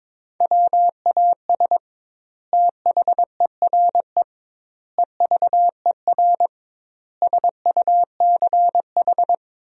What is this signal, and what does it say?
Morse code, heard as a keyed tone, about 22 wpm, 700 Hz. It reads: WAS THERE EVER SUCH
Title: 2x Word-Spacing - Sentences from Top 400 Words - 22wpm